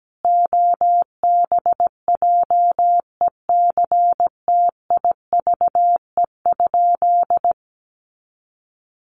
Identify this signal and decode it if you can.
Morse code, heard as a keyed tone, about 17 wpm, 700 Hz. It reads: OBJECTIVE?
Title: Sentences from Top 1200 Words 17wpm